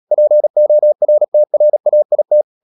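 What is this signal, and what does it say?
Morse code, heard as a keyed tone, about 37 wpm, 600 Hz. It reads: PORTRAIT
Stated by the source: Morse code practice at 35–39 wpm sending English text of Ken Follett's novel Never, from CW med Gnister.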